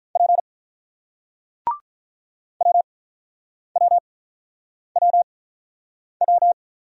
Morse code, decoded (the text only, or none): P WWWW